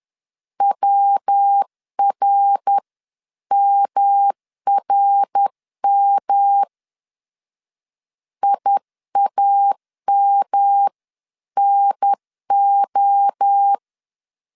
Morse code, heard as a keyed tone, about 10 wpm, 800 Hz. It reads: WRMRM IAMNO
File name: 10wpm